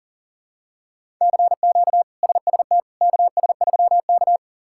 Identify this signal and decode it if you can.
Morse code, heard as a keyed tone, about 40 wpm, 700 Hz. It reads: CQ SST KS3K
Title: K1USN SST QSO Practice - 40wpm